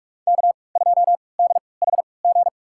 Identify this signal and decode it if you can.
Morse code, heard as a keyed tone, about 46 wpm, spaced as for 25 wpm, 700 Hz. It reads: K2DHG
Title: USA Call Signs - 5 Characters - 3x Character Spacing - 45wpm